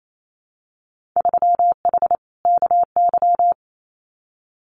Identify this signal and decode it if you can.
Morse code, heard as a keyed tone, about 28 wpm, 700 Hz. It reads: 3H KY
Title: Winter Field Day Exchanges Encore 28wpm